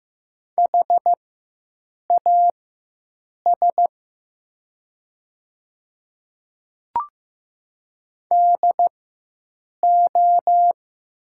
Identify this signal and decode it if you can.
Morse code, heard as a keyed tone, about 15 wpm, spaced as for 7 wpm, 700 Hz. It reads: HAS DO
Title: 4x Character Spacing - Top 100 Words - 15wpm